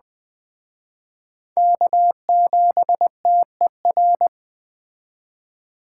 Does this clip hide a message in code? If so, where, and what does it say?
Yes: Morse code from 1.6 to 4.3 s, heard as a keyed tone, about 21 wpm, 700 Hz. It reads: K7TER